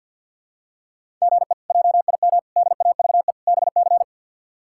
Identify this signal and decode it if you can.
Morse code, heard as a keyed tone, about 52 wpm, 690 Hz. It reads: GE JIM DAVE BC
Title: K1USN SST QSO Practice - 50wpm